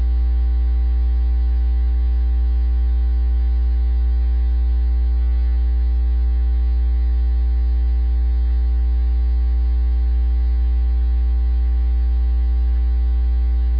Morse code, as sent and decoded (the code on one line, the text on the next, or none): none